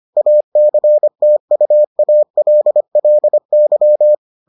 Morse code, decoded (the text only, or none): ACTUALLY